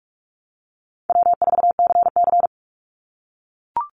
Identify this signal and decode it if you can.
Morse code, heard as a keyed tone, about 44 wpm, 700 Hz. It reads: W4CC
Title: Easy Call Signs - 45wpm